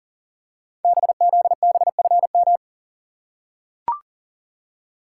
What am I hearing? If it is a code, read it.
Morse code, heard as a keyed tone, about 38 wpm, 690 Hz. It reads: BZBFM
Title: Code Groups - 5 Letters - 40wpm